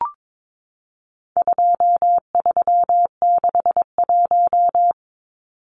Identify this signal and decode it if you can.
Morse code, heard as a keyed tone, about 22 wpm, 700 Hz. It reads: 2361
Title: Numbers - 4-Digit 22wpm